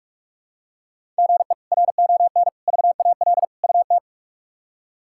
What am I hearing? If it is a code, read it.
Morse code, heard as a keyed tone, about 46 wpm, 700 Hz. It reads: GE RON VAL UT
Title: K1USN SST QSO Practice - 45wpm